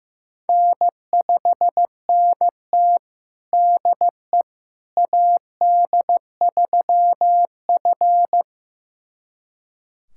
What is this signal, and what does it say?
Morse code, heard as a keyed tone, about 15 wpm, 700 Hz. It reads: N5NT DE AD3F
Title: USA Callsign DE Callsign - 4 Characters - 15wpm